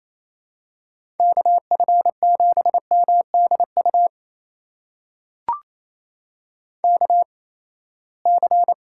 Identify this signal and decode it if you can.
Morse code, heard as a keyed tone, about 28 wpm, spaced as for 19 wpm, 700 Hz. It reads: KF7MDU KC
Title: USA Call Signs - 6 Characters - 8x Character Spacing - 28wpm